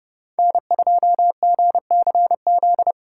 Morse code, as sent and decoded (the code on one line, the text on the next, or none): -. ..--- --. -.-. --..
N2GCZ